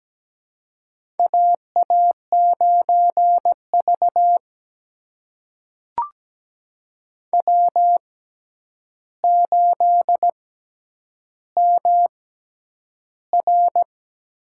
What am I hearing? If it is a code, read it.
Morse code, heard as a keyed tone, about 17 wpm, spaced as for 9 wpm, 700 Hz. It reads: AA9V W8MR
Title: USA Call Signs - 4 Characters - 6x Character Spacing - 17wpm